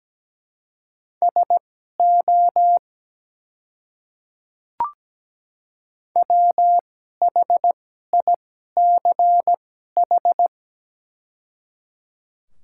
Morse code, heard as a keyed tone, about 17 wpm, spaced as for 12 wpm, 700 Hz. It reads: SO WHICH
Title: 2x Character Spacing - Top 100 Words - 17wpm